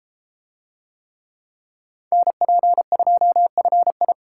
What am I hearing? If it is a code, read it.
Morse code, heard as a keyed tone, about 32 wpm, 700 Hz. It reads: NP2FI